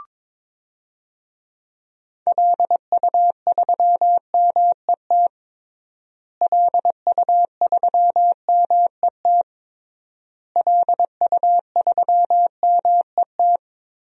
Morse code, heard as a keyed tone, about 22 wpm, 700 Hz. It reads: LU3MET LU3MET LU3MET
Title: DXCC-100 Call Signs - 6 Characters - Repeated 3x - 22wpm